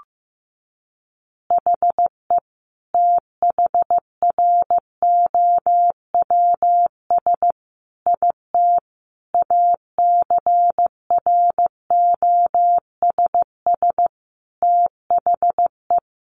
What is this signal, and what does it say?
Morse code, heard as a keyed tone, about 15 wpm, 700 Hz. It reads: HE THROWS IT ACROSS THE
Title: Additional Easy Phrases 15wpm